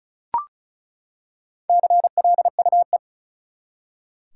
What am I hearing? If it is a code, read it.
Morse code, heard as a keyed tone, about 36 wpm, 700 Hz. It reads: CLUE
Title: Four Letter Words - 35wpm